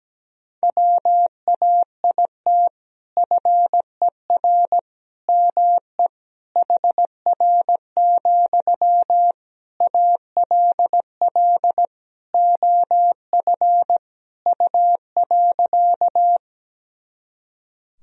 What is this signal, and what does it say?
Morse code, heard as a keyed tone, about 17 wpm, 700 Hz. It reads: WAIT FER ME HR, ALL OF U.
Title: Sentences from Top 500 Words and Abbreviations 17wpm